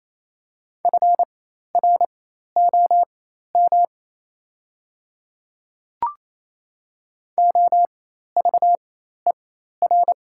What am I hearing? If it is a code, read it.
Morse code, heard as a keyed tone, about 27 wpm, spaced as for 13 wpm, 690 Hz. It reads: FROM OVER
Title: 4x Character Spacing - Top 100 Words - 28wpm